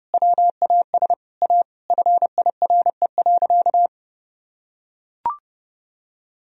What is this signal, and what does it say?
Morse code, heard as a keyed tone, about 30 wpm, 700 Hz. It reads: WAS A FIRE.
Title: Sentences from Top 1000 Words - Review Part 4 - 30wpm